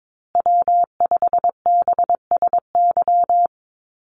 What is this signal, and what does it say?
Morse code, heard as a keyed tone, about 22 wpm, 700 Hz. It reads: W5BSY
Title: USA Call Signs Repeated 2x 22wpm